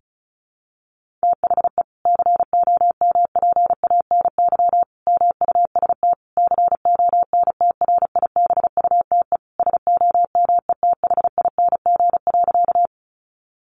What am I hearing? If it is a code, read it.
Morse code, heard as a keyed tone, about 34 wpm, 700 Hz. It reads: THE COMPANY MUST CONTRIBUTE SOMETHING.
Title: Sentences from Top 1400 Words 35wpm